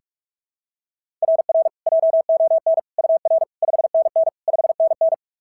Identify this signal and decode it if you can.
Morse code, heard as a keyed tone, about 44 wpm, 690 Hz. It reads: RR JON UR 5NN 5NN
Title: SOTA QSO Practice 45wpm